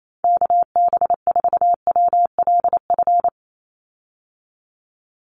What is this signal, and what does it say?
Morse code, heard as a keyed tone, about 28 wpm, 700 Hz. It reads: KB4WLF
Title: Call Signs - 28wpm